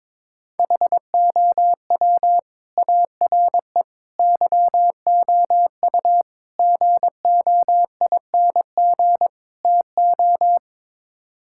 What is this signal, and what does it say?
Morse code, heard as a keyed tone, about 22 wpm, 700 Hz. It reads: HOW ARE YOU GOING TO
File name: practice / 22wpm / Build A Phrase - 5-Word Phrases - 22wpm